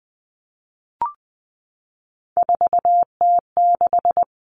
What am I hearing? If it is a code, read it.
Morse code, heard as a keyed tone, about 20 wpm, 700 Hz. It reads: 4T6
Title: Common Cut Numbers - 3-Digit 20wpm